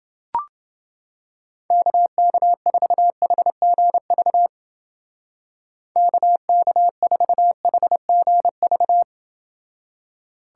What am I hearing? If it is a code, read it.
Morse code, heard as a keyed tone, about 29 wpm, 700 Hz. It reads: KK4HGV KK4HGV